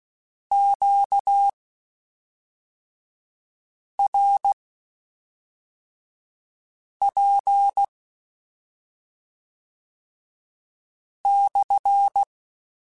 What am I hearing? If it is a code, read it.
Morse code, heard as a keyed tone, about 16 wpm, spaced as for 3 wpm, 800 Hz. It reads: QRP /